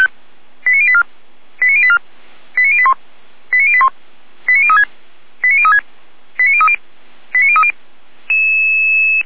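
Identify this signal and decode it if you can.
Five-tone selective call (ZVEI), heard as quick runs of five tones, tones of about 70 ms. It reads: 89084 89084 89081 89081 89036 89036 89030 89030
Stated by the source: ZVEI five-tone call 89127 received on an SDR by feuerwehr-orlamuende.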